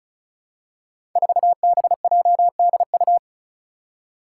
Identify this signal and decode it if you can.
Morse code, heard as a keyed tone, about 36 wpm, 700 Hz. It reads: 4BJDU
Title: Five Challenging Letter-Number Combinations 35wpm